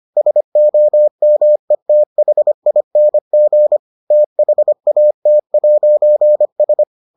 Morse code, heard as a keyed tone, about 25 wpm, 600 Hz. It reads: SOMETHING THAT'S